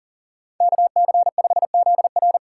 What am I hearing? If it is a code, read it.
Morse code, heard as a keyed tone, about 40 wpm, 700 Hz. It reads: KC5ZR
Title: USA Call Signs - 5 Characters - 3x Character Spacing - 40wpm